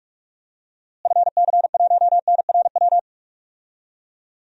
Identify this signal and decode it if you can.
Morse code, heard as a keyed tone, about 44 wpm, 700 Hz. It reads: FC1NRW
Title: DXCC-100 Call Signs - 6 Characters - Repeated 3x - 45wpm